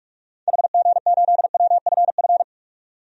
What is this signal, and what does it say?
Morse code, heard as a keyed tone, about 42 wpm, 700 Hz. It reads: HG8WFF
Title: DXCC-100 Call Signs - 6 Characters - 45wpm